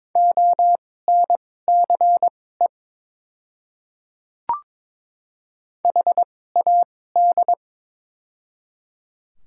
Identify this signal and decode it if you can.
Morse code, heard as a keyed tone, about 22 wpm, spaced as for 16 wpm, 700 Hz. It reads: ONCE HAD
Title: 2x Character Spacing - Top 300 Words - 22wpm